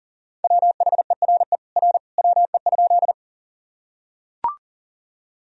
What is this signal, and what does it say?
Morse code, heard as a keyed tone, about 40 wpm, 700 Hz. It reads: WHERE R WE?
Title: Sentences from Top 200 Words and Abbreviations 40wpm